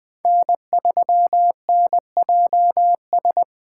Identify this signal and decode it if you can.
Morse code, heard as a keyed tone, about 20 wpm, 700 Hz. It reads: N3NJS
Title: Speed Racing - USA 1x3 Call Signs 17wpm